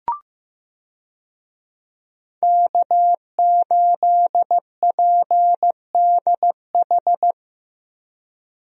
Morse code, heard as a keyed tone, about 15 wpm, 700 Hz. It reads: K8PDH
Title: USA Call Signs - 5 Characters - 15wpm